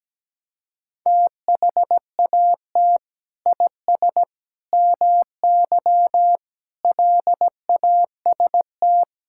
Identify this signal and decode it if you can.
Morse code, heard as a keyed tone, about 17 wpm, 700 Hz. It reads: THAT IS MY LAST
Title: Sentences from Top 1600 Words 17wpm